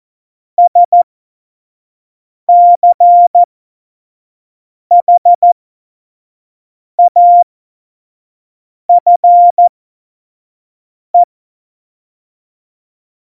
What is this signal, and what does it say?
Morse code, heard as a keyed tone, about 14 wpm, spaced as for 5 wpm, 700 Hz. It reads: SCHAFE
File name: Morsekurs / Witze / Witz 04. Februar 2020, 25 F70